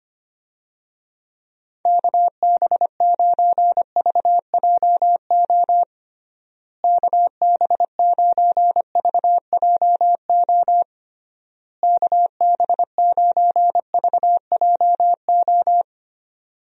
Morse code, heard as a keyed tone, about 25 wpm, 700 Hz. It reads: KB9VJO KB9VJO KB9VJO
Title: USA Call Signs - 6 Characters - Repeated 3x - 25wpm